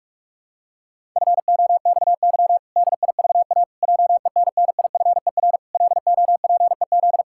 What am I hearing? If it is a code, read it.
Morse code, heard as a keyed tone, about 46 wpm, 700 Hz. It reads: FOXY DIVA JENNIFER LOPEZ
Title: Pangram Challenge v2 - 45wpm